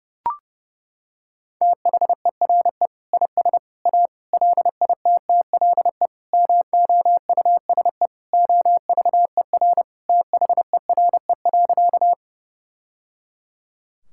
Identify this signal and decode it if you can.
Morse code, heard as a keyed tone, about 30 wpm, 700 Hz. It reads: THERE IS A LITTLE MOUSE OVER THERE.
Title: Sentences from Top 1000 Words - Review Part 5 - 30wpm